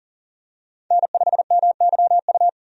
Morse code, heard as a keyed tone, about 40 wpm, 700 Hz. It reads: N5MYU
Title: USA Call Signs - 5 Characters - 40wpm